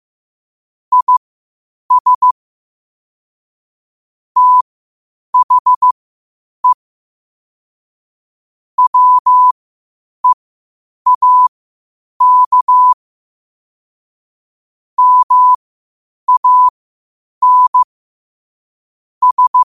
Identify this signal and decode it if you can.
Morse code, heard as a keyed tone, about 15 wpm, spaced as for 8 wpm, 1000 Hz. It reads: IS THE WEAK MAN S
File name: qotdfiles / 8wpm